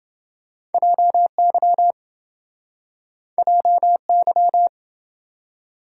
Morse code, heard as a keyed tone, about 28 wpm, 700 Hz. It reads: JY JY